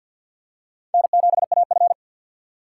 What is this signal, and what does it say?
Morse code, heard as a keyed tone, about 52 wpm, 700 Hz. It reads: N7AF
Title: USA Call Signs - 4 Characters - 3x Character Spacing - 50wpm